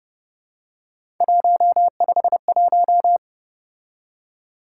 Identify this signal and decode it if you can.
Morse code, heard as a keyed tone, about 29 wpm, 700 Hz. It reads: 151
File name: Speed-Racing - Three Letter-Number Combinations - 25wpm